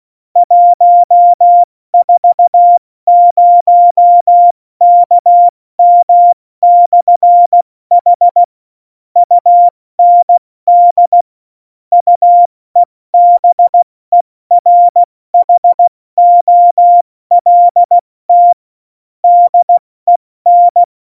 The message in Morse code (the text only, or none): 140KM/H UND UEBERHOLT DEN